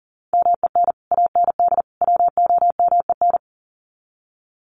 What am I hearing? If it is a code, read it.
Morse code, heard as a keyed tone, about 40 wpm, 700 Hz. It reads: MEN AND WOMEN
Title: Sets of 3 Words 40wpm